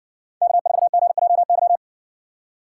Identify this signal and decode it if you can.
Morse code, heard as a keyed tone, about 60 wpm, 700 Hz. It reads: K4GJY